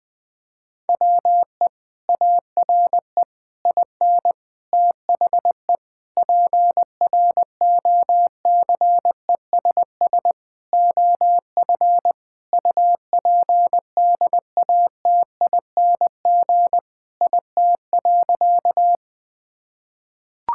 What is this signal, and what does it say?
Morse code, heard as a keyed tone, about 20 wpm, 700 Hz. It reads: WE ARE IN THE PROCESS OF UPDATING IT.
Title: Sentences with 6-Word Phrases - 20wpm